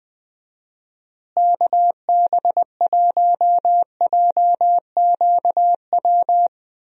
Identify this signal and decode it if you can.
Morse code, heard as a keyed tone, about 21 wpm, 700 Hz. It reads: KB1JQW
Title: USA Call Signs - 6 Characters - Repeated 3x - 20wpm